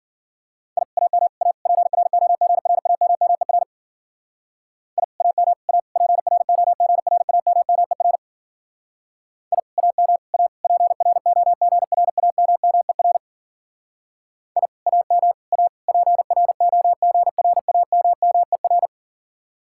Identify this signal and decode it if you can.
Morse code, heard as a keyed tone, about 50 wpm, 700 Hz. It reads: I AM A PROGRAMMER I AM A PROGRAMMER I AM A PROGRAMMER I AM A PROGRAMMER
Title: Speed-Racing Common QSO Phrases - 40wpm